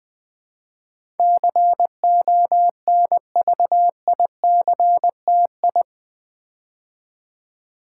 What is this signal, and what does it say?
Morse code, heard as a keyed tone, about 20 wpm, 700 Hz. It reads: CONVICTI
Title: Build A Word - 4 Letters At A Time - 10-Letter Words - 20wpm